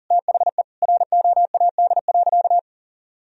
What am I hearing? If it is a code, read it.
Morse code, heard as a keyed tone, about 40 wpm, 700 Hz. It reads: THE ROAD.
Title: Sentences from Top 1000 Words - Review Part 2 - 40wpm